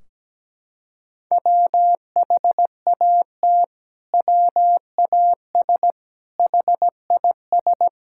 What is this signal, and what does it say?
Morse code, heard as a keyed tone, about 17 wpm, 700 Hz. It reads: WHAT WAS HIS
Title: Sentences from Top 1600 Words 17wpm